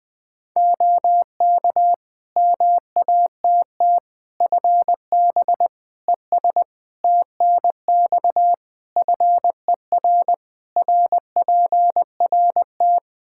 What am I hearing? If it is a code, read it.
Morse code, heard as a keyed tone, about 20 wpm, 700 Hz. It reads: OK MATT FB ES TNX FER RPRT
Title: Practice QSO - Exchange 2 - Info - 20wpm